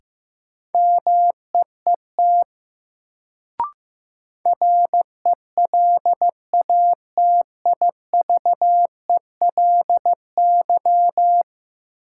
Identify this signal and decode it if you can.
Morse code, heard as a keyed tone, about 15 wpm, 700 Hz. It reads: MEET RELATIVELY
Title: Sentences from Top 1100 Words - New Word Drill Alt - 15wpm